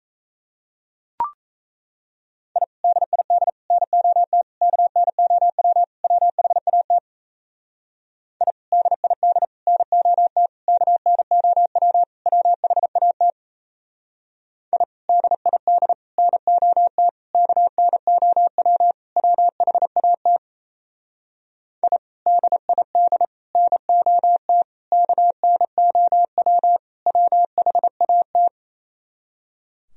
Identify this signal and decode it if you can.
Morse code, heard as a keyed tone, about 35 wpm, 700 Hz. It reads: I DID NOT KNOW WHAT I DID NOT KNOW WHAT I DID NOT KNOW WHAT I DID NOT KNOW WHAT